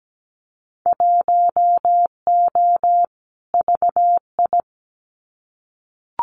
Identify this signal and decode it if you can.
Morse code, heard as a keyed tone, about 17 wpm, 700 Hz. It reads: 1O VI
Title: Winter Field Day Exchanges Encore 17wpm